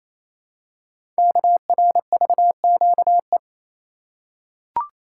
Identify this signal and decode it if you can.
Morse code, heard as a keyed tone, about 28 wpm, 700 Hz. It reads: KRVQE
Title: Code Groups - 5 Letters - 28wpm